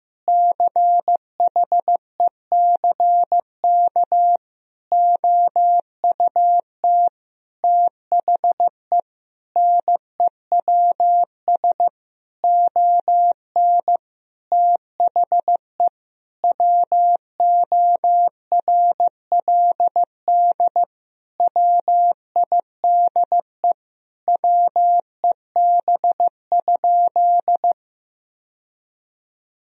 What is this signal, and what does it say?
Morse code, heard as a keyed tone, about 15 wpm, 700 Hz. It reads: CHECK OUT THE NEWS ON THE WORLD WIDE WEB?
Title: Sentences from Top 1000 Words - Review Part 4 - 15wpm